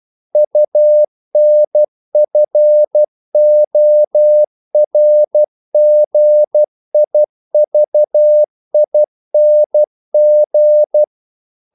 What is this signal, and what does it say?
Morse code, heard as a keyed tone, about 12 wpm, 600 Hz. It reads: UNFORGIVING
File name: CW / Practice / If12